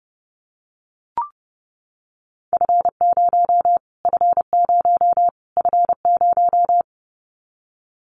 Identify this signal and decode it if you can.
Morse code, heard as a keyed tone, about 30 wpm, 700 Hz. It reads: F0 F0 F0
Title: Callsign Prefixes for Top 25 DXCC - Repeated 3x - 30wpm